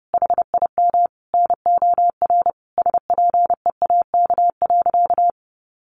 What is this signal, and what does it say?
Morse code, heard as a keyed tone, about 30 wpm, 700 Hz. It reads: HIM NOR SPEAK.
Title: Sentences from Top 1500 Words 30wpm